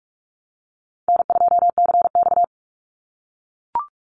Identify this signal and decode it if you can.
Morse code, heard as a keyed tone, about 44 wpm, 700 Hz. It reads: N2CX
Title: Easy Call Signs - 45wpm